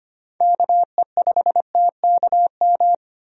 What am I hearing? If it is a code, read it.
Morse code, heard as a keyed tone, about 25 wpm, 700 Hz. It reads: KE5TKM
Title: USA Call Signs - 6 Characters - 25wpm